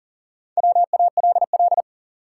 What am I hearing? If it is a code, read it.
Morse code, heard as a keyed tone, about 40 wpm, 700 Hz. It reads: WALL
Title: Sentences from Top 1000 Words - Review Part 6 - New Word Drill - 40wpm